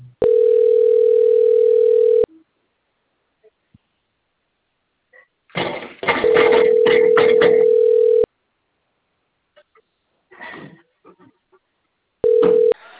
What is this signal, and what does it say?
Ringback tone (North America), 440 + 480 Hz.